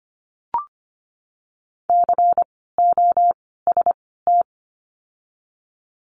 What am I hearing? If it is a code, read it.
Morse code, heard as a keyed tone, about 24 wpm, 700 Hz. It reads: C O S T
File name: course / 25:10wpm / 094 Lesson 16f - Letter c - Words 25wpm